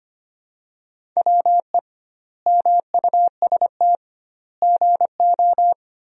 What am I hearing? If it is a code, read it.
Morse code, heard as a keyed tone, about 25 wpm, 700 Hz. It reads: WE MUST GO
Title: 2x Word-Spacing - Sentences from Top 300 Words - 25wpm